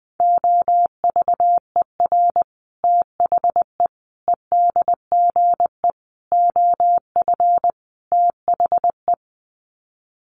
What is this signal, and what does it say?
Morse code, heard as a keyed tone, about 20 wpm, 700 Hz. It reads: OVER THE EDGE OF THE
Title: Sets of 5 Words Encore2 20wpm